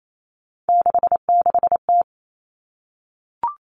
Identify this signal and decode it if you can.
Morse code, heard as a keyed tone, about 28 wpm, 700 Hz. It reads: 66T
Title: Common Cut Numbers - 3-Digit 28wpm